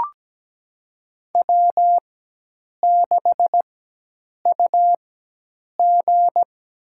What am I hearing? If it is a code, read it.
Morse code, heard as a keyed tone, about 17 wpm, spaced as for 8 wpm, 700 Hz. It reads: W6UG
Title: USA Call Signs - 4 Characters - 4x Character Spacing - 17wpm